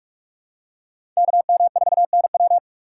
Morse code, heard as a keyed tone, about 41 wpm, 700 Hz. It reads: KM4NW
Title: USA Call Signs - 5 Characters - 3x Character Spacing - 45wpm